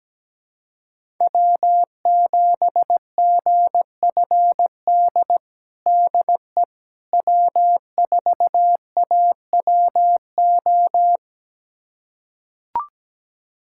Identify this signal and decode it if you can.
Morse code, heard as a keyed tone, about 17 wpm, 700 Hz. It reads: W7GFD DE W4AWO
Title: USA Callsign DE Callsign - 5 Characters - 17wpm